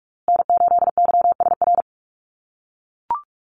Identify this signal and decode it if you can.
Morse code, heard as a keyed tone, about 44 wpm, 700 Hz. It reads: N8YSR
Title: USA Call Signs Repeated 3x 45wpm